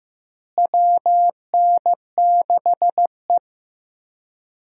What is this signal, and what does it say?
Morse code, heard as a keyed tone, about 15 wpm, 700 Hz. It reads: WN6E
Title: USA Call Signs - 4 Characters - Repeated 2x - 15wpm